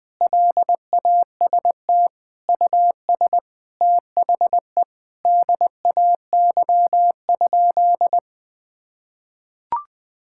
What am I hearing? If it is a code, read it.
Morse code, heard as a keyed tone, about 21 wpm, 700 Hz. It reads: LAST US THE DAY?
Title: Sentences from Top 1000 Words - Review Part 5 - 20wpm